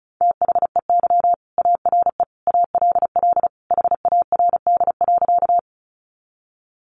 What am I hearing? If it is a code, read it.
Morse code, heard as a keyed tone, about 34 wpm, 700 Hz. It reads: THEY ARE ALL HARD.
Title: Additional Easy Phrases 35wpm